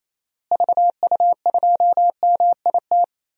Morse code, heard as a keyed tone, about 28 wpm, 700 Hz. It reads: VU2MIT
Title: DXCC-100 Call Signs - 6 Characters - Repeated 3x - 28wpm